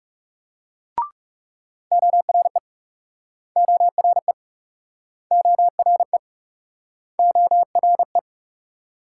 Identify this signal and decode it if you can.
Morse code, heard as a keyed tone, about 44 wpm, 700 Hz. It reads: ORE ORE ORE ORE